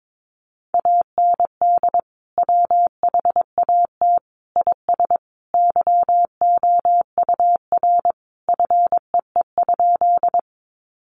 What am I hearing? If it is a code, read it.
Morse code, heard as a keyed tone, about 22 wpm, 700 Hz. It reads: AND WHAT IS YOUR FEE?
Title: Sentences from Top 1500 Words 22wpm